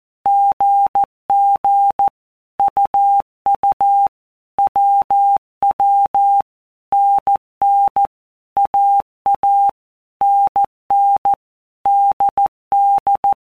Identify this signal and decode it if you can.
Morse code, heard as a keyed tone, about 14 wpm, 800 Hz. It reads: GG UU WW NN AA NN DD
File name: Basic Letter Groups By Blocks / Groups@13 / AWUNDG@13